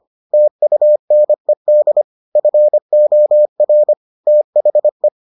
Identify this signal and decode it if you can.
Morse code, heard as a keyed tone, about 25 wpm, 600 Hz. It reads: TUNED FOR THE